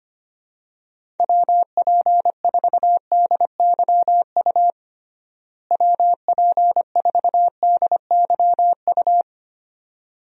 Morse code, heard as a keyed tone, about 25 wpm, 700 Hz. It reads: WP4DYU WP4DYU
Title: USA Call Signs - 6 Characters - Repeated 2x - 25wpm